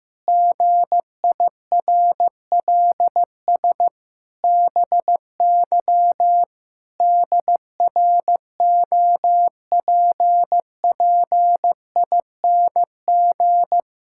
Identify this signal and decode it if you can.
Morse code, heard as a keyed tone, about 15 wpm, 700 Hz. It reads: GIRLS BY DROPPING
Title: Pangram Challenge v2 - 15wpm